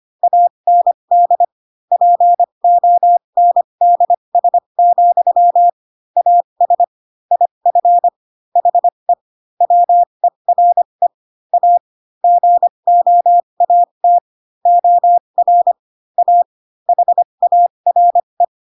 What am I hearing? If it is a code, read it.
Morse code, heard as a keyed tone, about 25 wpm, spaced as for 22 wpm, 700 Hz. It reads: AND PONDS, AS IF HE WERE A GOAT OR A HARE
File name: Pinocchio / Pinocchio ch4 18 25 ik1hge